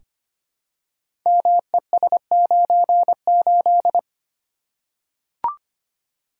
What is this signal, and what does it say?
Morse code, heard as a keyed tone, about 25 wpm, 700 Hz. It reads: MES98
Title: Code Groups - 5 Letter:Numbers - 25wpm